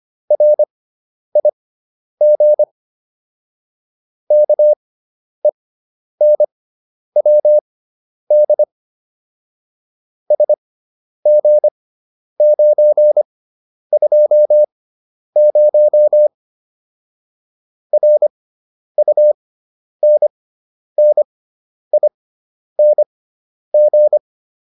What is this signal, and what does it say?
Morse code, heard as a keyed tone, about 25 wpm, spaced as for 10 wpm, 600 Hz. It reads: RIG KENWD SG920 RUNNING